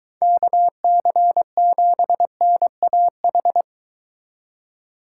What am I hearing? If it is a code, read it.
Morse code, heard as a keyed tone, about 23 wpm, 700 Hz. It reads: KC7NAH